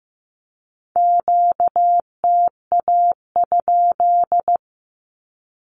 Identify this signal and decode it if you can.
Morse code, heard as a keyed tone, about 15 wpm, 700 Hz. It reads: QTA?
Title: Q-Codes with Spoken Meaning - 15wpm